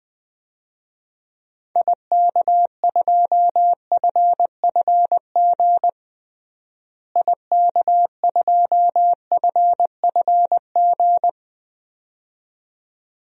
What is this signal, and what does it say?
Morse code, heard as a keyed tone, about 20 wpm, 700 Hz. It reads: IK2FFG IK2FFG
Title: DXCC-100 Call Signs - 6 Characters - Repeated 2x - 20wpm